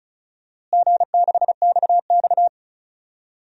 Morse code, heard as a keyed tone, about 35 wpm, 700 Hz. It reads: G6XX
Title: DXCC-100 Call Signs - 4 Characters - Repeated 2x - 35wpm